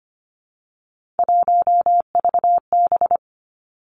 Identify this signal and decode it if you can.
Morse code, heard as a keyed tone, about 25 wpm, 700 Hz. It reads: 1VB
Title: Three Challenging Letter-Number Combinations 25wpm